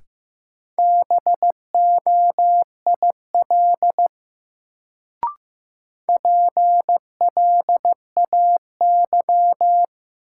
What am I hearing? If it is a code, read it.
Morse code, heard as a keyed tone, about 15 wpm, 700 Hz. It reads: BOIL PLAY
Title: Sets of 5 Words Encore - New Word Drill - 15wpm